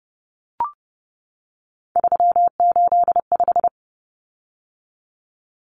Morse code, heard as a keyed tone, about 30 wpm, 700 Hz. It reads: 385
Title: Common Cut Numbers - 3-Digit 30wpm